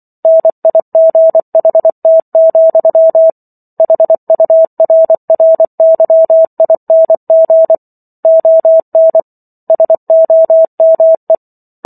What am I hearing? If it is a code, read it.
Morse code, heard as a keyed tone, about 24 wpm, 650 Hz. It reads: NIGHT, HURRYING ON SOME